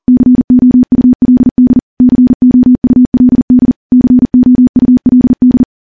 Morse code, heard as a keyed tone, about 40 wpm, 250 Hz. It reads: COULD COULD COULD
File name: morse-training-40WPM